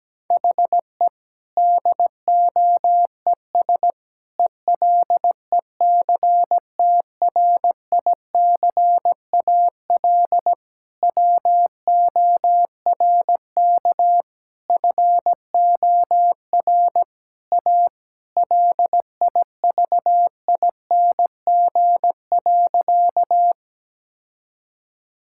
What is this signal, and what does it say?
Morse code, heard as a keyed tone, about 17 wpm, 700 Hz. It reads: HE DOES ELECTRICAL WORK FOR A LIVING.
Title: Sentences from Top 1000 Words - Review Part 6 - 17wpm